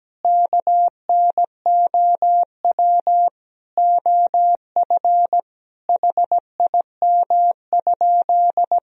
Morse code, heard as a keyed tone, about 17 wpm, 700 Hz. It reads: KNOW OF HIM?
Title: Sentences from Top 100 Words and Abbreviations 17wpm